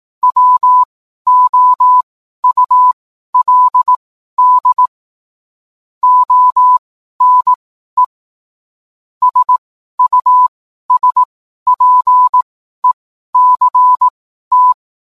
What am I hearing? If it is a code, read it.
Morse code, heard as a keyed tone, about 18 wpm, spaced as for 12 wpm, 1000 Hz. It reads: WOULD ONE SUSPECT